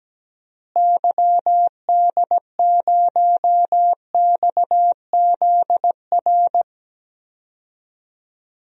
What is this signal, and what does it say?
Morse code, heard as a keyed tone, about 17 wpm, 700 Hz. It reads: YD0XZR
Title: DXCC-100 Call Signs - 6 Characters - Repeated 2x - 17wpm